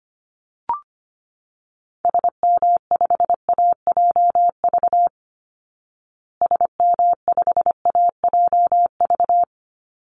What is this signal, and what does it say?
Morse code, heard as a keyed tone, about 25 wpm, 700 Hz. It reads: SM5AJV SM5AJV
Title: Call Signs Repeated 3x - 25wpm